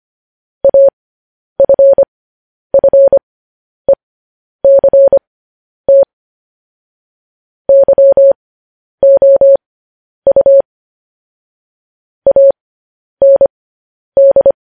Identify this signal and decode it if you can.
Morse code, heard as a keyed tone, about 25 wpm, spaced as for 10 wpm, 550 Hz. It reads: AFFECT YOU AND
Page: news-headlines-in-morse-code-at-10-wpm